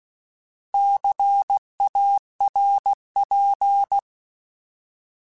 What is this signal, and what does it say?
Morse code, heard as a keyed tone, about 16 wpm, 800 Hz. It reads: CARP
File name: Alphabetical / Alphabetical mp3s / Cs @15